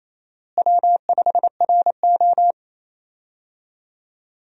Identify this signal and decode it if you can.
Morse code, heard as a keyed tone, about 28 wpm, 700 Hz. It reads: W5RO
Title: USA Call Signs - 4 Characters - Repeated 2x - 28wpm